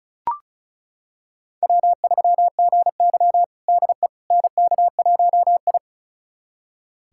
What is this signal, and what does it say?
Morse code, heard as a keyed tone, about 36 wpm, 700 Hz. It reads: W3GY DE NK1I
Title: USA Callsign DE Callsign - 4 Characters - 35wpm